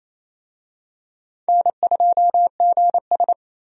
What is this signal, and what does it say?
Morse code, heard as a keyed tone, about 28 wpm, 700 Hz. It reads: N2GS
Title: USA Call Signs - 4 Characters - 28wpm